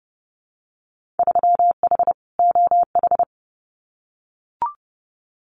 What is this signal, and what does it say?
Morse code, heard as a keyed tone, about 30 wpm, 700 Hz. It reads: 3H OH
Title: Winter Field Day Exchanges Encore 30wpm